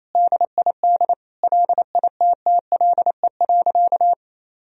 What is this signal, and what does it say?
Morse code, heard as a keyed tone, about 28 wpm, 700 Hz. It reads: DID LITTLE.
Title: Sentences from Top 1000 Words - Review Part 1 - 28wpm